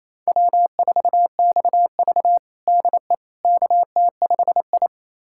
Morse code, heard as a keyed tone, about 28 wpm, 700 Hz. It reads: W4XV DE KT5I
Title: USA Callsign DE Callsign - 4 Characters - 28wpm